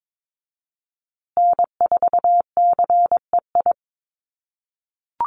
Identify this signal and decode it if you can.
Morse code, heard as a keyed tone, about 22 wpm, 700 Hz. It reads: N4CEI
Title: USA Call Signs Repeated 2x 22wpm